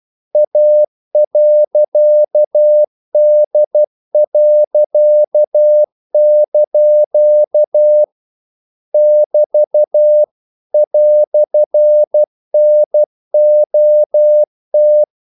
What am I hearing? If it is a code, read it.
Morse code, heard as a keyed tone, about 12 wpm, 600 Hz. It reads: A.D.) <BT> "NOT
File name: CW / Practice / EnglishQuotations0312